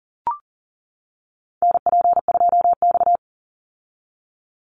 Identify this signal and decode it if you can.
Morse code, heard as a keyed tone, about 41 wpm, 700 Hz. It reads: NP2X
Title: Call Signs - 40wpm